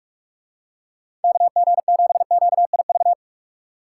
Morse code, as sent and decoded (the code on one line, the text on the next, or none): -.- --. --... --.- .. ...-
KG7QIV